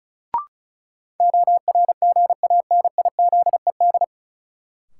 Morse code, heard as a keyed tone, about 34 wpm, 700 Hz. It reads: ORGANIZED
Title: Sentences from Top 1300 Words - New Word Drill Alt - 35wpm